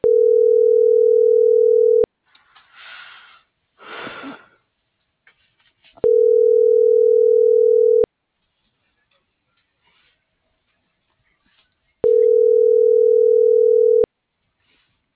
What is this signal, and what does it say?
Ringback tone (North America), 440 + 480 Hz.